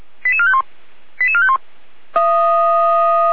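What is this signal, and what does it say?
Five-tone selective call (ZVEI), heard as quick runs of five tones, tones of about 80 ms. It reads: 88451 88451
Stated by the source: ZVEI five-tone call 88447 received on SDR, Feuerwehr Orlamünde.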